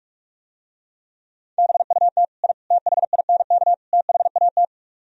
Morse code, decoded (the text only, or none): BUT I THINK THAT